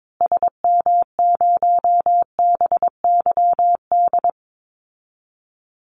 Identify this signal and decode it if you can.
Morse code, heard as a keyed tone, about 22 wpm, 700 Hz. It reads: SM0BYD
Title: Call Signs Repeated 2x - 22wpm